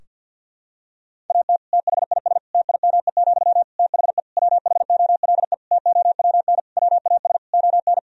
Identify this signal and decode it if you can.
Morse code, heard as a keyed tone, about 52 wpm, 700 Hz. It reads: AT THIS TIME, THE WHOLE TOWN WAS ON
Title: Sentences from Top 1000 Words - Review Part 2 - 50wpm